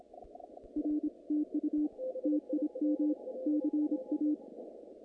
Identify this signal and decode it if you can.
Morse code, heard as a keyed tone, about 27 wpm, 300 Hz. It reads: R TU TIM CA
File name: contests / naqp-cw-k5tr-op-k5pi- -2300